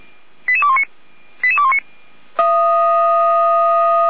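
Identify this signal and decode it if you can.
Five-tone selective call (ZVEI), heard as quick runs of five tones, tones of about 70 ms. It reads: 88219 88219
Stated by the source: Five-tone ZVEI selective call received by SDR.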